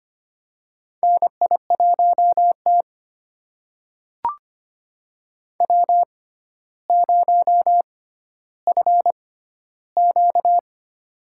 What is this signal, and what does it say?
Morse code, heard as a keyed tone, about 25 wpm, spaced as for 13 wpm, 700 Hz. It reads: NI1T W0FQ
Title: USA Call Signs - 4 Characters - 6x Character Spacing - 25wpm